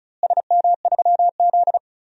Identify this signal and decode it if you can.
Morse code, heard as a keyed tone, about 36 wpm, 700 Hz. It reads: SM3Z
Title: DXCC-100 Call Signs - 4 Characters - Repeated 2x - 35wpm